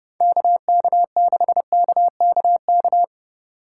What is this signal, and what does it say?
Morse code, heard as a keyed tone, about 29 wpm, 700 Hz. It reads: KK6KKK